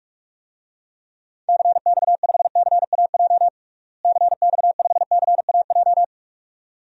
Morse code, heard as a keyed tone, about 41 wpm, 700 Hz. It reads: CX5CAJ CX5CAJ
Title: DXCC-100 Call Signs - 6 Characters - Repeated 2x - 45wpm